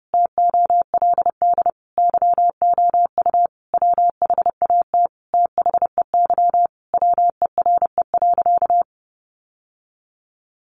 Morse code, heard as a keyed tone, about 30 wpm, 700 Hz. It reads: TOLD YOU WHAT THEY WERE.
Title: Sentences from Top 1000 Words 30wpm